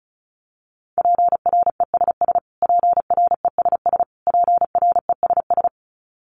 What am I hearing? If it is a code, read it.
Morse code, heard as a keyed tone, about 34 wpm, 700 Hz. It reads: PRESS PRESS PRESS
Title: Top 500 Words Repeated 35wpm